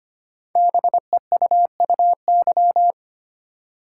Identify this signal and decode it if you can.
Morse code, heard as a keyed tone, about 25 wpm, 700 Hz. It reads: BEUUY
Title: Code Groups - 5 Letters - 25wpm